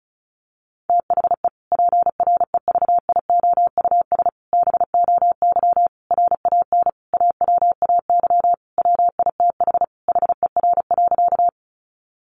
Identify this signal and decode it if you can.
Morse code, heard as a keyed tone, about 34 wpm, 700 Hz. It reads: THE PREVIOUS BOY RAN AWAY WITH HER.
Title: Sentences from Top 1500 Words 35wpm